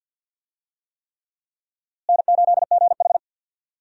Morse code, heard as a keyed tone, about 48 wpm, 700 Hz. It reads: N8GH
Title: USA Call Signs - 4 Characters - Repeated 3x - 50wpm